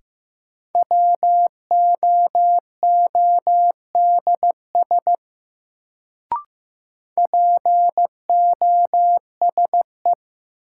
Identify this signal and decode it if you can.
Morse code, heard as a keyed tone, about 15 wpm, 700 Hz. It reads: WOODS POSE
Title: Sentences from Top 1000 Words - Review Part 2 - New Word Drill - 15wpm